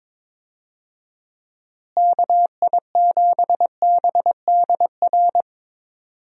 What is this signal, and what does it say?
Morse code, heard as a keyed tone, about 21 wpm, 700 Hz. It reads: KI7BDR
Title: USA Call Signs - 6 Characters - Repeated 3x - 22wpm